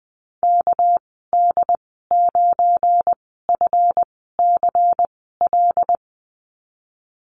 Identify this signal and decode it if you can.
Morse code, heard as a keyed tone, about 20 wpm, spaced as for 14 wpm, 700 Hz. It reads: KD9FCL
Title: 231 Lesson 37g - Letter x - Call Signs 20wpm